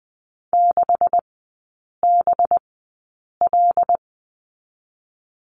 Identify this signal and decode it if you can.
Morse code, heard as a keyed tone, about 20 wpm, spaced as for 8 wpm, 700 Hz. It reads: 6BL
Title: 198 Lesson 32e - Letter k - Three Characters 20wpm